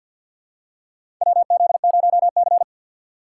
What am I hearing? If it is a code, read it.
Morse code, heard as a keyed tone, about 47 wpm, 700 Hz. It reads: WZ0C
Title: USA Call Signs - 4 Characters - 6x Character Spacing - 50wpm